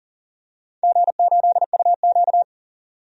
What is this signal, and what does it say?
Morse code, heard as a keyed tone, about 37 wpm, 700 Hz. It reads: G8UQ
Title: DXCC-100 Call Signs - 4 Characters - 40wpm